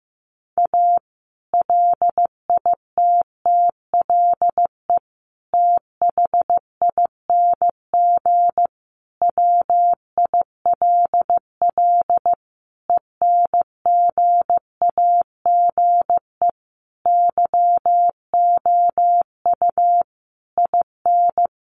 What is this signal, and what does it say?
Morse code, heard as a keyed tone, about 15 wpm, 700 Hz. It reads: A LITTLE THING WILL ENGAGE YOU IN
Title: Sentences from Top 1500 Words 15wpm